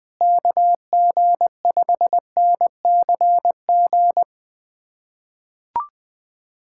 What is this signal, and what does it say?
Morse code, heard as a keyed tone, about 20 wpm, 700 Hz. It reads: KG5NCG
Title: USA Callsign DE Callsign 20wpm